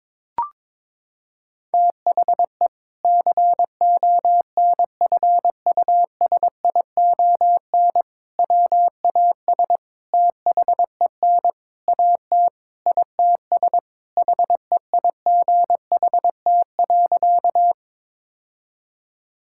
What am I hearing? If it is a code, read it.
Morse code, heard as a keyed tone, about 22 wpm, 700 Hz. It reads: THE CONFUSION WAS THEN AT ITS HEIGHT.
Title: Sentences from Top 1600 Words 22wpm